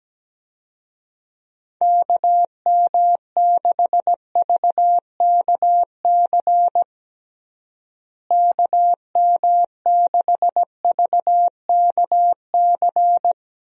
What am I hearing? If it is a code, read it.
Morse code, heard as a keyed tone, about 17 wpm, 700 Hz. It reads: KM6VKC KM6VKC
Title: USA Call Signs - 6 Characters - Repeated 2x - 17wpm